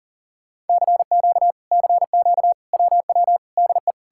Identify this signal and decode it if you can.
Morse code, heard as a keyed tone, about 53 wpm, 700 Hz. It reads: CQ CQ WW DE